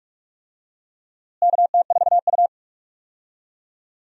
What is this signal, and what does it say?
Morse code, heard as a keyed tone, about 42 wpm, 700 Hz. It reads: KT4U
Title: USA Call Signs - 4 Characters - Repeated 3x - 45wpm